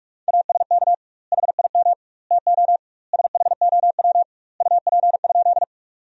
Morse code, heard as a keyed tone, about 41 wpm, 700 Hz. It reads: ASK HIM TO SHOW UP?
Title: 2x Word-Spacing - Sentences from Top 200 Words - 45wpm